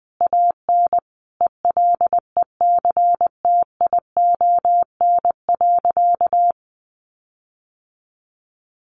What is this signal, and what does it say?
Morse code, heard as a keyed tone, about 20 wpm, 700 Hz. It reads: AN ELECTION.
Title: Sentences from Top 1500 Words 20wpm